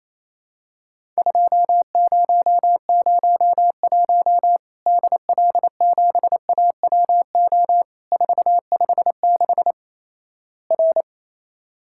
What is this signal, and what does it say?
Morse code, heard as a keyed tone, about 28 wpm, 700 Hz. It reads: 2001 DL7AWO 456 R
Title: Worked All Europe DX Contest - QTC Exchange - 28wpm